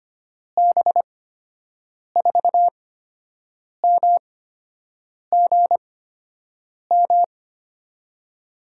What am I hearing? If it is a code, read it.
Morse code, heard as a keyed tone, about 25 wpm, spaced as for 7 wpm, 700 Hz. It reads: B4MGM
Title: USA Call Signs - 6 Characters - 8x Character Spacing - 25wpm